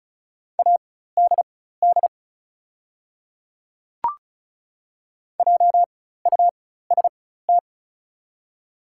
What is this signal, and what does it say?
Morse code, heard as a keyed tone, about 36 wpm, spaced as for 16 wpm, 700 Hz. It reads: ADD JUST
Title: 4x Character Spacing - Top 200 Words - 35wpm